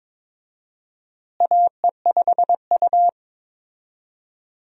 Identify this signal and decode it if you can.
Morse code, heard as a keyed tone, about 22 wpm, 700 Hz. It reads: AE5U
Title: USA Call Signs - 4 Characters - Repeated 2x - 22wpm